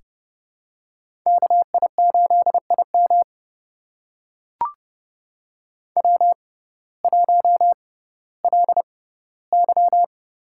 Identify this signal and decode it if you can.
Morse code, heard as a keyed tone, about 31 wpm, spaced as for 17 wpm, 700 Hz. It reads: KI8IM W1LY
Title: USA Call Signs - 5 Characters - 6x Character Spacing - 30wpm